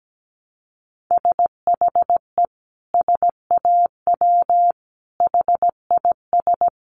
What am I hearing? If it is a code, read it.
Morse code, heard as a keyed tone, about 17 wpm, 700 Hz. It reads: SHE SAW HIS
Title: Sentences from Top 1300 Words 17wpm